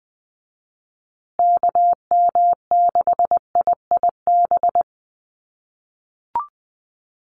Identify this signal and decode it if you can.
Morse code, heard as a keyed tone, about 20 wpm, 700 Hz. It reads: KM6IIB
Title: USA Call Signs Repeated 2x 20wpm